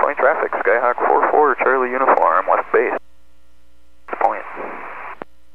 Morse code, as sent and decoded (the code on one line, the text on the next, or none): none